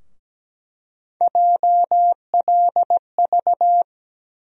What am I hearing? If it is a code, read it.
Morse code, heard as a keyed tone, about 17 wpm, 700 Hz. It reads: JLV
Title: USA Call Sign Postfixes - 3 Letters 17wpm